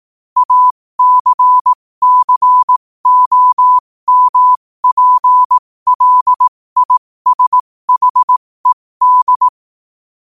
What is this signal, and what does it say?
Morse code, heard as a keyed tone, about 18 wpm, spaced as for 15 wpm, 1000 Hz. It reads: ACCOMPLISHED
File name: 15wpm